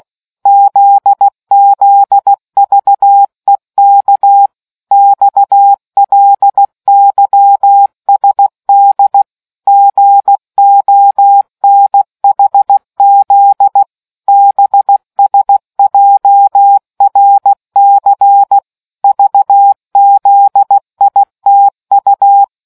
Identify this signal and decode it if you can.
Morse code, heard as a keyed tone, about 16 wpm, 800 Hz. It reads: ZZVEK XLYSD GONHZ BSJRC VZITU